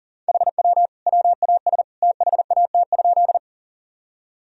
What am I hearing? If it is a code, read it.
Morse code, heard as a keyed tone, about 40 wpm, 700 Hz. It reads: HW WAS THAT?
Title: Sentences from Top 100 Words and Abbreviations 40wpm